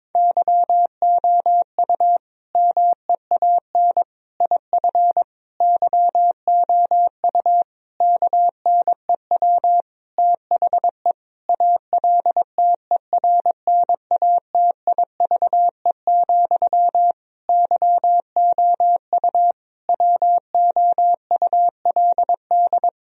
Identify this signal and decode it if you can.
Morse code, heard as a keyed tone, about 22 wpm, 700 Hz. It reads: YOU MEAN IF YOU KNEW THE ALTERNATIVE, YOU WOULD
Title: Sentences from Top 1000 Words - Review Part 4 - 22wpm